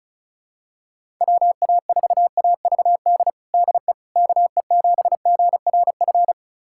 Morse code, heard as a keyed tone, about 36 wpm, 690 Hz. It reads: WA4AVD DE KE7GRF
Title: USA Callsign DE Callsign 35wpm